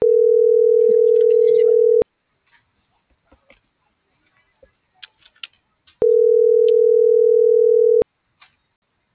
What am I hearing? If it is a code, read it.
Ringback tone (North America), 440 + 480 Hz.